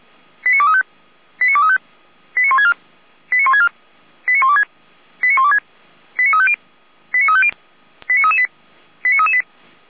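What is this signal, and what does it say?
Five-tone selective call (ZVEI), heard as quick runs of five tones, tones of about 70 ms. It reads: 89235 89235 89164 89164 89126 89126 89350 89350 89338 89338
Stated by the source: ZVEI five-tone selective call received by SDR.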